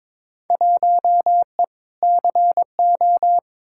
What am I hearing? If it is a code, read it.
Morse code, heard as a keyed tone, about 22 wpm, 700 Hz. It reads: 1E CO
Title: Field Day Exchanges - 22wpm